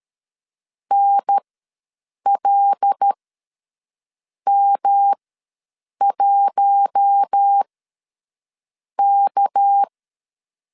Morse code, heard as a keyed tone, about 13 wpm, spaced as for 6 wpm, 800 Hz. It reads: NLM1K
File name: LEZIONI / 7 GF9Z / 7 GF9 ws 12wpm 3344